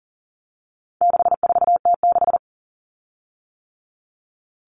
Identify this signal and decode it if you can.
Morse code, heard as a keyed tone, about 40 wpm, 700 Hz. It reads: 64T6
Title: Common Cut Numbers - 4-Digit 40wpm